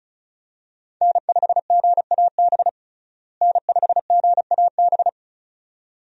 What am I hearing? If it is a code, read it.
Morse code, heard as a keyed tone, about 34 wpm, 700 Hz. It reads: N5GAB N5GAB
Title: USA Call Signs - 5 Characters - Repeated 2x - 35wpm